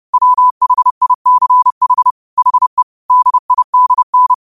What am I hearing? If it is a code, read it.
Morse code, heard as a keyed tone, about 30 wpm, 1000 Hz. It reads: WHICH HE DIDN